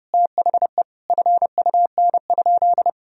Morse code, heard as a keyed tone, about 30 wpm, 700 Hz. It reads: THE FUN?
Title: Sentences from Top 1000 Words - Review Part 3 - 30wpm